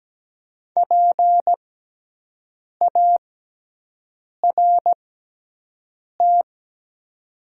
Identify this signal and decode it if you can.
Morse code, heard as a keyed tone, about 17 wpm, spaced as for 6 wpm, 700 Hz. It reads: PART
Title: 6x Character Spacing - Top 200 Words - 17wpm